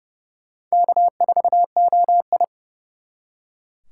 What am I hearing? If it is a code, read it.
Morse code, heard as a keyed tone, about 35 wpm, 700 Hz. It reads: K4OI
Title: Speed Racing - USA 1x2 Call Signs 30wpm